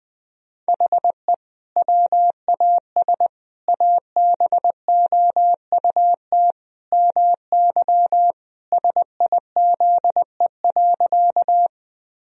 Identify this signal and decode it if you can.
Morse code, heard as a keyed tone, about 21 wpm, 700 Hz. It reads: HE WAS ABOUT MY SIZE.
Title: Sentences from Top 1000 Words - Review Part 2 - 20wpm